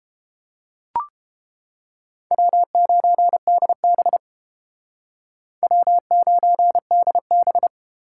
Morse code, heard as a keyed tone, about 30 wpm, 700 Hz. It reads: W9DB W9DB